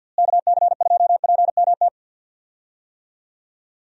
Morse code, heard as a keyed tone, about 48 wpm, 700 Hz. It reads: KC2PMT
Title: USA Call Signs - 6 Characters - Repeated 2x - 50wpm